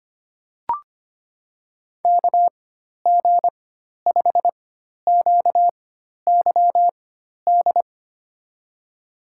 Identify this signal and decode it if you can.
Morse code, heard as a keyed tone, about 25 wpm, spaced as for 12 wpm, 700 Hz. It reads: KG5QYD
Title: USA Call Signs - 4x Character Spacing - 25wpm